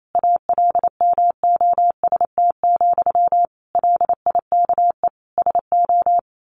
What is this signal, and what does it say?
Morse code, heard as a keyed tone, about 28 wpm, 700 Hz. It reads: ALMOST, LIKE SO
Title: A Sound Of Thunder 28wpm